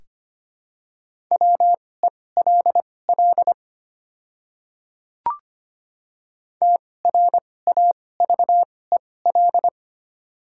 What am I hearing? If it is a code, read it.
Morse code, heard as a keyed tone, about 25 wpm, spaced as for 18 wpm, 690 Hz. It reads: WELL TRAVEL
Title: 2x Character Spacing - Top 500 Words - 25wpm